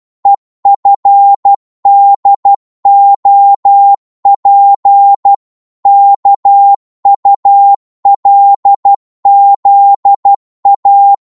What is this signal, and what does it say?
Morse code, heard as a keyed tone, about 12 wpm, 800 Hz. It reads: EFDOP KULZA
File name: cwmp3 / ltr-12wpm